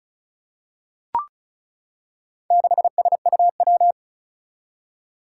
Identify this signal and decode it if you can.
Morse code, heard as a keyed tone, about 36 wpm, 700 Hz. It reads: 6SUW
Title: Four Challenging Letter-Number Combinations 35wpm